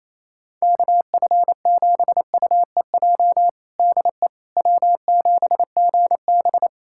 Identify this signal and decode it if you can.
Morse code, heard as a keyed tone, about 28 wpm, 700 Hz. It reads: KF7UEJ DE W7GB
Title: Practice QSO - Exchange 4 - Ending - 28wpm